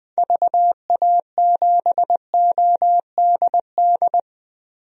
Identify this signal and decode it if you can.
Morse code, heard as a keyed tone, about 20 wpm, 700 Hz. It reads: VA7ODD
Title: DXCC-100 Call Signs - 6 Characters - Repeated 3x - 20wpm